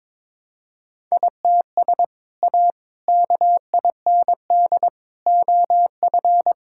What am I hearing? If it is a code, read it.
Morse code, heard as a keyed tone, about 22 wpm, 700 Hz. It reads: ITS A KIND OF
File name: practice / 22wpm / Sentences with 3-Word Phrases - 22wpm